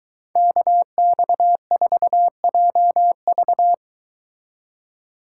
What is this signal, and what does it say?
Morse code, heard as a keyed tone, about 23 wpm, 700 Hz. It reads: KX4JV